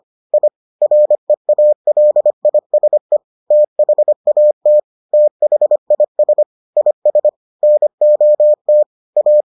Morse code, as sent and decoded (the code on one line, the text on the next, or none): .. / .-. .- .-.. .. ... / - .... .- - / - .... .. ... / .. ... / -. --- - / .-
I REALISE THAT THIS IS NOT A